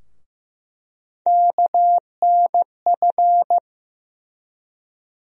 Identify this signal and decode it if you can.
Morse code, heard as a keyed tone, about 15 wpm, 700 Hz. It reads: KNF
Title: USA Call Sign Postfixes - 3 Letters 15wpm